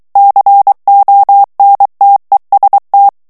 Morse code, heard as a keyed tone, about 23 wpm, 800 Hz. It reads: CONTEST
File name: ITALIEN texte clair 2 vitesse 22 mots minute